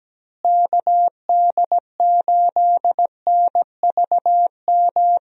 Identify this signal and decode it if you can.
Morse code, heard as a keyed tone, about 17 wpm, 700 Hz. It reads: KD8NVM
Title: USA Call Signs - 6 Characters - Repeated 2x - 17wpm